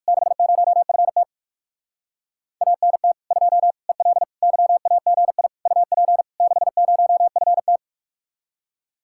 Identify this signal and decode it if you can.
Morse code, heard as a keyed tone, about 59 wpm, 700 Hz. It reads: 60FT ANT 2 EL YAGI UP 60FT